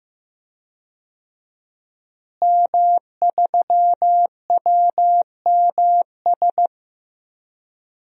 Morse code, heard as a keyed tone, about 15 wpm, 700 Hz. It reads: M3WMS
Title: DXCC-100 Call Signs - 5 Characters - Repeated 3x - 15wpm